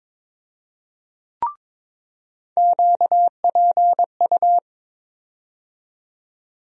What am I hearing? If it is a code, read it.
Morse code, heard as a keyed tone, about 22 wpm, 700 Hz. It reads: QPU
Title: USA Call Sign Postfixes - 3 Letters 22wpm